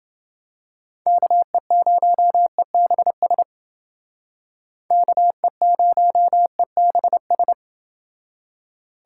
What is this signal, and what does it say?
Morse code, heard as a keyed tone, about 29 wpm, 700 Hz. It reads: KE0EBS KE0EBS